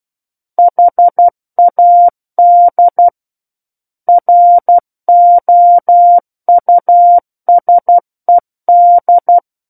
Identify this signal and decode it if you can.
Morse code, heard as a keyed tone, about 12 wpm, 700 Hz. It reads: HAD ROUSED
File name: Book One / 03-Chapter 3 - 12 WPM